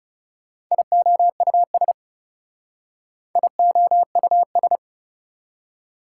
Morse code, heard as a keyed tone, about 38 wpm, 700 Hz. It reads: IOUS IOUS